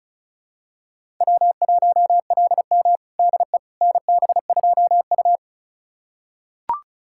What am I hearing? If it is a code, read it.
Morse code, heard as a keyed tone, about 36 wpm, 700 Hz. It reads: W1LM DE NB2U
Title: USA Callsign DE Callsign - 4 Characters - 35wpm